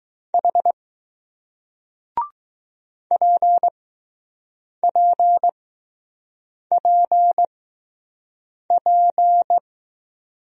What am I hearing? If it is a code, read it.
Morse code, heard as a keyed tone, about 20 wpm, 700 Hz. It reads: H P P P P